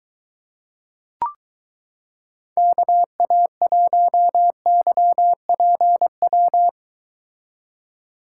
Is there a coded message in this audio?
Yes: Morse code from 2.6 to 6.7 s, heard as a keyed tone, about 22 wpm, 700 Hz. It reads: KA1YPW